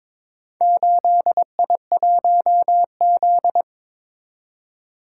Morse code, heard as a keyed tone, about 22 wpm, 700 Hz. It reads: 8I1Z
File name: practice / 22wpm / Four Letter-Number Combinations - 22wpm